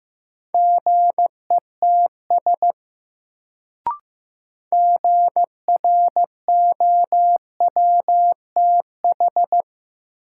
Morse code, heard as a keyed tone, about 15 wpm, 700 Hz. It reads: GETS GROWTH
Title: Sentences from Top 1300 Words - New Word Drill Alt - 15wpm